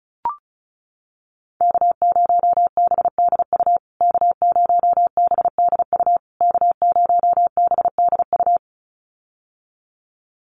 Morse code, heard as a keyed tone, about 34 wpm, 700 Hz. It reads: K0BDU K0BDU K0BDU
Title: USA Call Signs Repeated 3x 35wpm